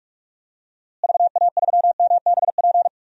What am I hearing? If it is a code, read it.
Morse code, heard as a keyed tone, about 45 wpm, 700 Hz. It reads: VA3MBP
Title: DXCC-100 Call Signs - 6 Characters - Repeated 2x - 45wpm